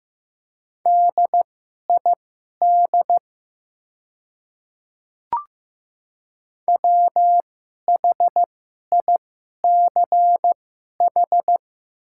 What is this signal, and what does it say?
Morse code, heard as a keyed tone, about 15 wpm, spaced as for 11 wpm, 700 Hz. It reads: DID WHICH
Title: 2x Character Spacing - Top 500 Words - 15wpm